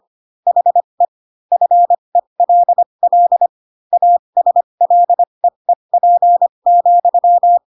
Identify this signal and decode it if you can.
Morse code, heard as a keyed tone, about 25 wpm, spaced as for 22 wpm, 700 Hz. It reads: HE FELL ASLEEP,
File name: Pinocchio / Pinocchio ch6 18 25 ik1hge